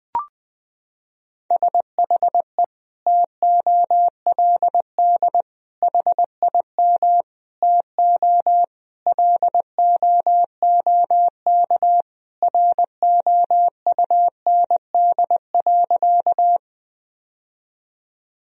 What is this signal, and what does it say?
Morse code, heard as a keyed tone, about 20 wpm, 700 Hz. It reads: SHE TOLD HIM TO LOOK ROUND.
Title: Sentences from Top 1000 Words - Review Part 1 - 20wpm